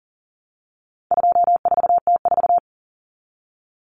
Morse code, heard as a keyed tone, about 40 wpm, 700 Hz. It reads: 24T4
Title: Common Cut Numbers - 4-Digit 40wpm